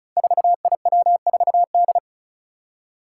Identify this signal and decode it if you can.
Morse code, heard as a keyed tone, about 36 wpm, 700 Hz. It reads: 4IW4D
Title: Five Challenging Letter-Number Combinations 35wpm